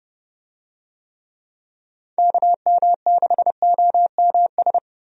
Morse code, heard as a keyed tone, about 30 wpm, 700 Hz. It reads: KM6OMS
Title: USA Call Signs - 6 Characters - Repeated 2x - 30wpm